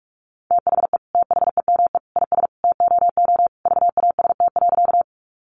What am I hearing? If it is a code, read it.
Morse code, heard as a keyed tone, about 44 wpm, 700 Hz. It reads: THE THEME IS TOO VAST.
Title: Sentences from Top 1300 Words 45wpm